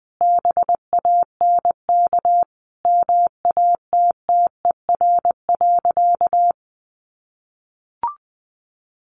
Morse code, heard as a keyed tone, about 20 wpm, 700 Hz. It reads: BANK MATTER.
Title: Sentences from Top 1400 Words 20wpm